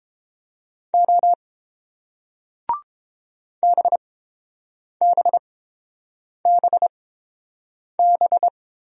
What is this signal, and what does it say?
Morse code, heard as a keyed tone, about 31 wpm, spaced as for 7 wpm, 700 Hz. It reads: O BBBB